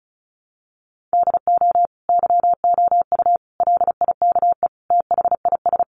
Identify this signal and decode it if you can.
Morse code, heard as a keyed tone, about 34 wpm, 700 Hz. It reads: DO YOU LIKE THIS
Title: Sentences from Top 1200 Words 35wpm